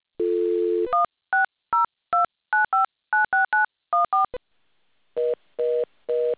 DTMF, heard as telephone touch tones, tones of about 150 ms. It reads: 16029596914